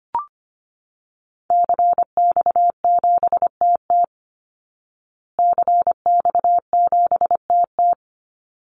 Morse code, heard as a keyed tone, about 25 wpm, 700 Hz. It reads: CX7TT CX7TT
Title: Call Signs Repeated 3x - 25wpm